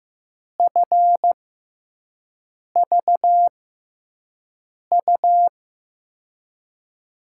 Morse code, heard as a keyed tone, about 15 wpm, spaced as for 5 wpm, 700 Hz. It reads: FVU
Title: USA Call Signs - 6 Characters - 6x Character Spacing - 15wpm